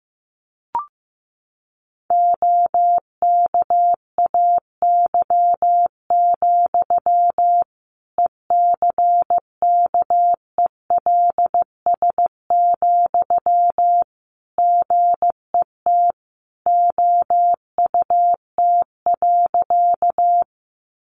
Morse code, heard as a keyed tone, about 15 wpm, 700 Hz. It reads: OKAY, ECKELS, GET OUT.
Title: A Sound Of Thunder 15wpm